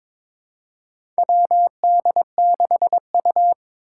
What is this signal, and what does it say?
Morse code, heard as a keyed tone, about 21 wpm, 700 Hz. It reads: WD6U
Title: USA Call Signs - 4 Characters - 2x Character Spacing - 22wpm